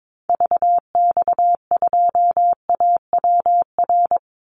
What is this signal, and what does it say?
Morse code, heard as a keyed tone, about 22 wpm, 700 Hz. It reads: VX2AWR
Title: Call Signs Repeated 2x - 22wpm